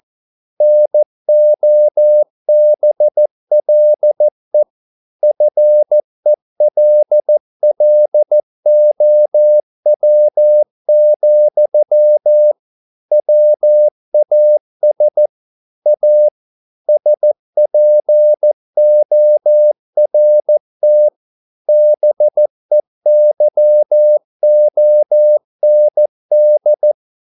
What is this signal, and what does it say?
Morse code, heard as a keyed tone, about 14 wpm, 600 Hz. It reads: NOBLE FELLOW, WAS A SPORT BEYOND
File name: mp3 files / a dogs mistake - 14wpm